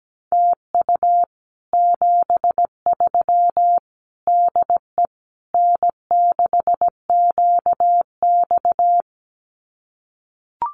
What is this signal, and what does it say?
Morse code, heard as a keyed tone, about 17 wpm, 700 Hz. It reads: TU 73 DE N6QX
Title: Common QSO Phrases Encore 17wpm